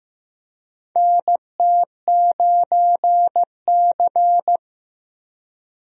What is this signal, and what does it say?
Morse code, heard as a keyed tone, about 15 wpm, 700 Hz. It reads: NT9C
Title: USA Call Signs - 4 Characters - Repeated 2x - 15wpm